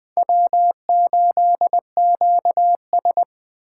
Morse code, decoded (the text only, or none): W8QS